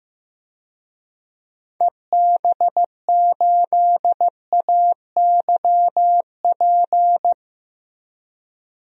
Morse code, heard as a keyed tone, about 16 wpm, 700 Hz. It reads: EB8AYP
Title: DXCC-100 Call Signs - 6 Characters - Repeated 3x - 15wpm